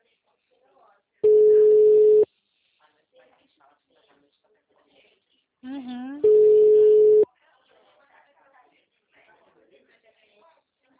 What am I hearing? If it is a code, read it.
Ringback tone (Europe, 425 Hz), with speech.
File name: out- -2013- - - .13023